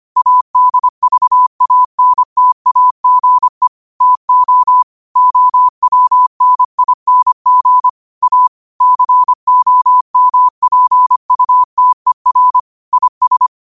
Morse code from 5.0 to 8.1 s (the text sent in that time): OWNING